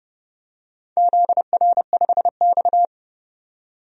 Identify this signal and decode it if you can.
Morse code, heard as a keyed tone, about 30 wpm, 700 Hz. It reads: ZR5X
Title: DXCC-100 Call Signs - 4 Characters - 30wpm